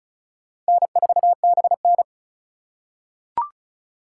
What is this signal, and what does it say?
Morse code, heard as a keyed tone, about 36 wpm, 700 Hz. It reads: N4BN
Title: USA Call Signs - 4 Characters - 2x Character Spacing - 35wpm